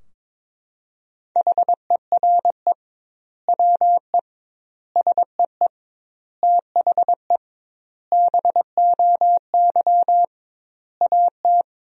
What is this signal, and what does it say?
Morse code, heard as a keyed tone, about 21 wpm, 700 Hz. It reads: HERE WE SEE THE BOY AT
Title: 2x Word-Spacing - Sentences from Top 300 Words - 22wpm